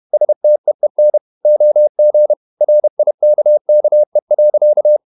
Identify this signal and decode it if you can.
Morse code, heard as a keyed tone, about 31 wpm, 600 Hz. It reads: STEEN OG RIKKE.
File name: Ferie QRQ 31wpm